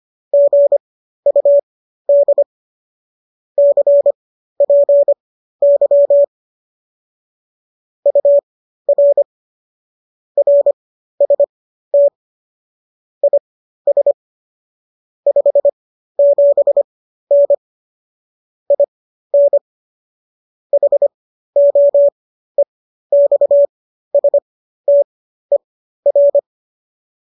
Morse code, heard as a keyed tone, about 25 wpm, spaced as for 13 wpm, 570 Hz. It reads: GUD CPY UR RST IS 57N IN HOEXSTER
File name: qso219 13